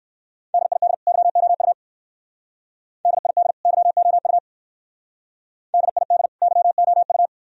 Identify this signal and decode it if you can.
Morse code, heard as a keyed tone, about 56 wpm, 690 Hz. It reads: DID YOU DID YOU DID YOU
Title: Speed-Racing - Sets of 2 Words 45wpm